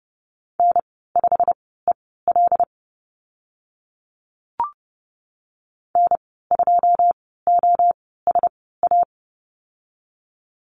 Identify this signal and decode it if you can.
Morse code, heard as a keyed tone, about 30 wpm, spaced as for 17 wpm, 700 Hz. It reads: N5EL N2OSA
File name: Lesson 15g - Number 5 - Call Signs 30wpm